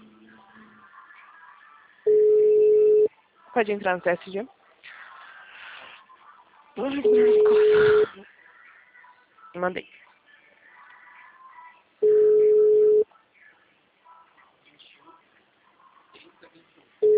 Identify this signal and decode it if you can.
Ringback tone (Europe, 425 Hz), with speech.
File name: out- -2005- - - .12